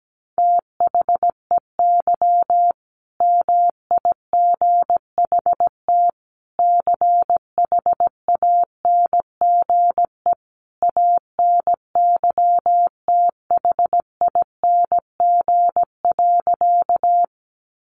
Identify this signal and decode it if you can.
Morse code, heard as a keyed tone, about 17 wpm, 700 Hz. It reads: THEY MIGHT CHANGE ANYTHING.
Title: A Sound Of Thunder 17wpm